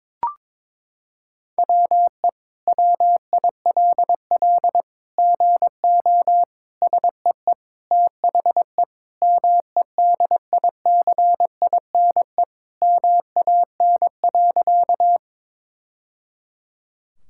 Morse code, heard as a keyed tone, about 22 wpm, 700 Hz. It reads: WE WILL GO SEE THE MEDICINE MAN.
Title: Sentences from Top 1000 Words - Review Part 6 - 22wpm